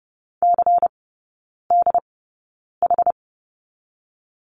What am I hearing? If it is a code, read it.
Morse code, heard as a keyed tone, about 30 wpm, spaced as for 9 wpm, 690 Hz. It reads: CDH
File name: Lesson 17e - Letter u - Three Characters 30wpm